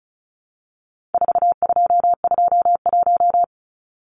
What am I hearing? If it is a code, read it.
Morse code, heard as a keyed tone, about 34 wpm, 700 Hz. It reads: 4221
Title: Numbers - 4-Digit 35wpm